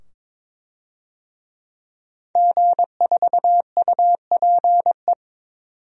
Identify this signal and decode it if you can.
Morse code, heard as a keyed tone, about 22 wpm, 700 Hz. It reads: G4UPE
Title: DXCC-100 Call Signs - 5 Characters - 22wpm